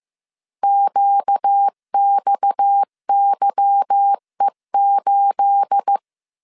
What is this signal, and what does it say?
Morse code, heard as a keyed tone, about 15 wpm, 800 Hz. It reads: QXYE8
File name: CW / DET / 1 9 1500C S A 14wpm 3222